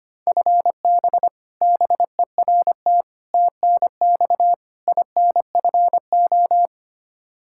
Morse code, heard as a keyed tone, about 25 wpm, 700 Hz. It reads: FB BERT TNX INFO
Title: Practice QSO - Exchange 3 - More Info - 25wpm